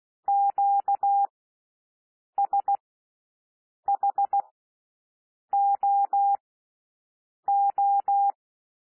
Morse code, heard as a keyed tone, about 16 wpm, spaced as for 6 wpm, 800 Hz. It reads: QSHOO
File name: cwmp3 / L1 SOUND LETTERS 16 wpm